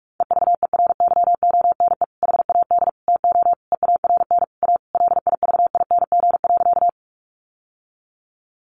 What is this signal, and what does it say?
Morse code, heard as a keyed tone, about 44 wpm, 700 Hz. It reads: EVERYONE HAD TO EARN A LIVING.
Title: Sentences from Top 1500 Words 45wpm